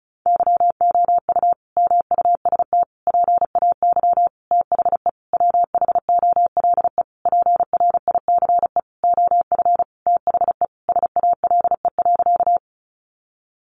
Morse code, heard as a keyed tone, about 34 wpm, 700 Hz. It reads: YOU MUST PAY THE WHOLE PRICE OF THE SALE.
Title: Sentences from Top 1400 Words 35wpm